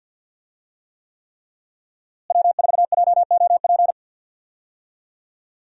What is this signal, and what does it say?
Morse code, heard as a keyed tone, about 52 wpm, 700 Hz. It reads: W4JOP
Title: USA Call Signs - 5 Characters - Repeated 3x - 50wpm